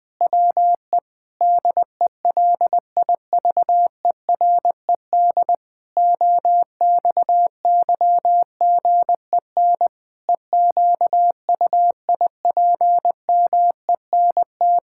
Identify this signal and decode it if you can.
Morse code, heard as a keyed tone, about 20 wpm, 700 Hz. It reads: WE DELIVERED OXYGEN EQUIPMENT
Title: Pangram Challenge v2 - 20wpm